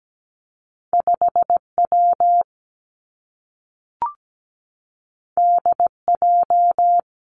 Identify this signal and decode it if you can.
Morse code, heard as a keyed tone, about 17 wpm, 700 Hz. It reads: 5W DJ
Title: Two Challenging Letter-Number Combinations 17wpm